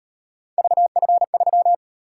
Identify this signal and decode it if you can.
Morse code, heard as a keyed tone, about 38 wpm, 700 Hz. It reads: VF3